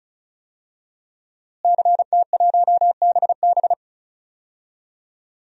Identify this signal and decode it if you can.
Morse code, heard as a keyed tone, about 36 wpm, 700 Hz. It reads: CT1BB
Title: DXCC-100 Call Signs - 5 Characters - Repeated 2x - 35wpm